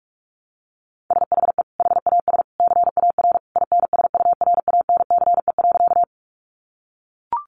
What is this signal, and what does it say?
Morse code, heard as a keyed tone, about 44 wpm, 700 Hz. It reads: SHE HAS CAR INSURANCE.
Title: Sentences from Top 1100 Words 45wpm